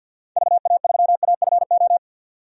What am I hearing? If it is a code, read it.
Morse code, heard as a keyed tone, about 47 wpm, 700 Hz. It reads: VA3AFO
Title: DXCC-100 Call Signs - 6 Characters - 50wpm